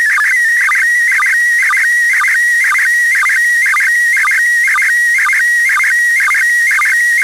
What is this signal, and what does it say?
SSTV (slow-scan television), PD 120.
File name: samples ogg / satnogs